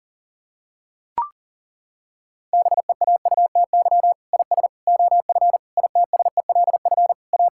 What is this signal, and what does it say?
Morse code, heard as a keyed tone, about 37 wpm, 700 Hz. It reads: BEAUTY IS OF ITSELF A
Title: Sentences from Top 1000 Words - Review Part 5 - 40wpm